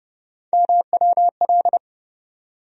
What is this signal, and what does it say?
Morse code, heard as a keyed tone, about 31 wpm, 700 Hz. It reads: MWL
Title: USA Call Sign Postfixes - 3 Letters 30wpm